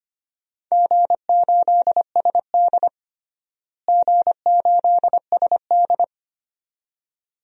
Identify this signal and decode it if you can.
Morse code, heard as a keyed tone, about 25 wpm, 700 Hz. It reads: G8SD G8SD
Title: DXCC-100 Call Signs - 4 Characters - Repeated 2x - 25wpm